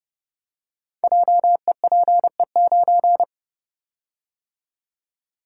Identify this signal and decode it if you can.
Morse code, heard as a keyed tone, about 29 wpm, 700 Hz. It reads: JEPE9
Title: Code Groups - 5 Letter:Numbers - 30wpm